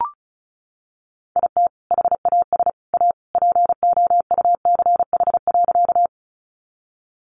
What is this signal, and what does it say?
Morse code, heard as a keyed tone, about 34 wpm, 700 Hz. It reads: IT HAS A POUCH.
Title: Easy Phrases 35wpm